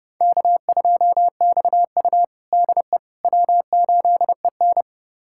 Morse code, heard as a keyed tone, about 30 wpm, 700 Hz. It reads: K2XU DE W8EN
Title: USA Callsign DE Callsign - 4 Characters - 30wpm